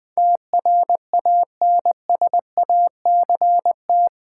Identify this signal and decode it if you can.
Morse code, heard as a keyed tone, about 20 wpm, 700 Hz. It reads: TRANSACT
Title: Build A Word - 4 Letters At A Time - 11-Letter Words - 20wpm